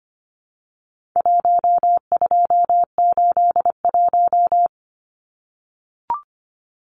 Morse code, heard as a keyed tone, about 25 wpm, 700 Hz. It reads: 1281
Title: Numbers - 4-Digit 25wpm